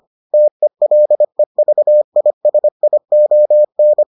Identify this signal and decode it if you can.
Morse code, heard as a keyed tone, about 25 wpm, 600 Hz. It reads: TELEVISION